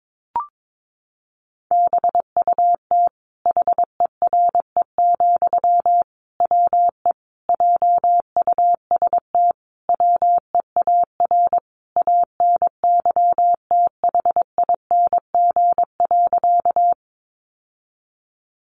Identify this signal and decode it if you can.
Morse code, heard as a keyed tone, about 22 wpm, 700 Hz. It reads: BUT HERE, WE JUST WEAR ANYTHING.
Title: Sentences from Top 1500 Words 22wpm